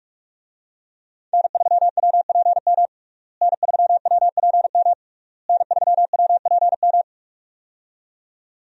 Morse code, heard as a keyed tone, about 46 wpm, 700 Hz. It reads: N3WPM N3WPM N3WPM
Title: USA Call Signs - 5 Characters - Repeated 3x - 45wpm